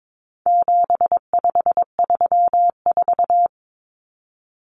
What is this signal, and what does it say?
Morse code, heard as a keyed tone, about 22 wpm, 700 Hz. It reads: 7534
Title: Common Cut Numbers - 4-Digit 22wpm